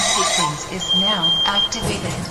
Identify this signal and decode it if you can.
Speech. There is no code.